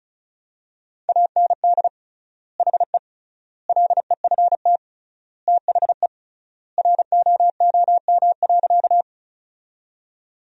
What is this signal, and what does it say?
Morse code, heard as a keyed tone, about 36 wpm, 700 Hz. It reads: AND HE LEFT THE ROOM.
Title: 3x Word-Spacing - Sentences from Top 300 Words - 35wpm